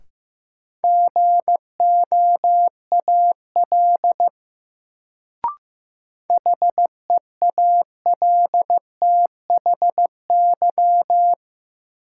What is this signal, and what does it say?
Morse code, heard as a keyed tone, about 15 wpm, 700 Hz. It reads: GOAL HEALTHY
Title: Sentences from Top 1400 Words - New Word Drill Alt - 15wpm